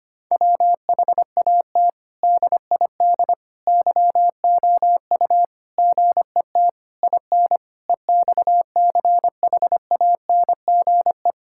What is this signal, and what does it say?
Morse code, heard as a keyed tone, about 25 wpm, 700 Hz. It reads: WHAT DID YOU GET IN EXCHANGE
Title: Sentences from Top 1000 Words - Review Part 6 - 25wpm